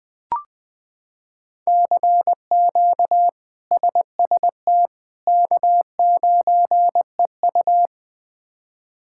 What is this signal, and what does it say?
Morse code, heard as a keyed tone, about 21 wpm, 700 Hz. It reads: CQ SST K9EU
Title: K1USN SST QSO Practice - 20wpm